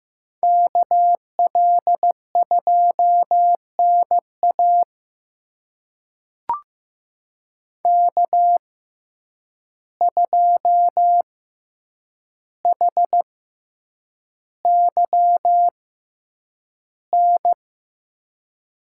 Morse code, heard as a keyed tone, about 15 wpm, spaced as for 8 wpm, 700 Hz. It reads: KL2NA K2HYN
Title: USA Call Signs - 5 Characters - 6x Character Spacing - 15wpm